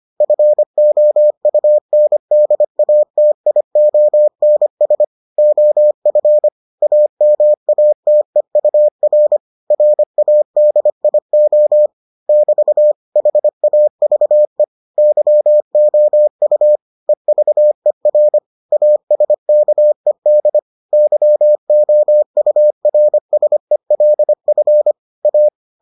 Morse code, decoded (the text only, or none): FOUNDATIONS OF AMATEUR RADIO <BT> HAVE YOU EVER ASKED YOURSELF A